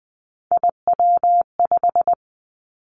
Morse code, heard as a keyed tone, about 20 wpm, 700 Hz. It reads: IW5
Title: Three Challenging Letter-Number Combinations 20wpm